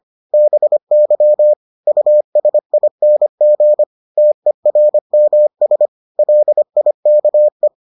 Morse code, heard as a keyed tone, about 25 wpm, 600 Hz. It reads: BY USING TERMS LIKE